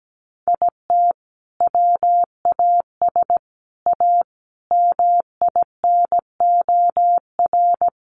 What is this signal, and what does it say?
Morse code, heard as a keyed tone, about 17 wpm, 700 Hz. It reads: IT WAS A MINOR
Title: Sentences from Top 1500 Words 17wpm